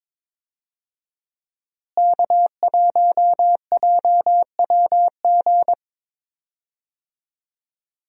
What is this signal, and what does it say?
Morse code, heard as a keyed tone, about 22 wpm, 700 Hz. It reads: K1JWG
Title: USA Call Signs - 5 Characters - Repeated 3x - 22wpm